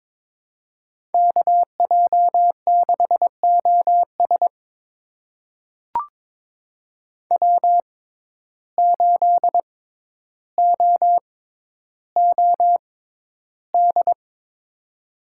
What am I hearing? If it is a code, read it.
Morse code, heard as a keyed tone, about 22 wpm, spaced as for 11 wpm, 700 Hz. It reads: KJ6OS W8OOD
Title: USA Call Signs - 5 Characters - 6x Character Spacing - 22wpm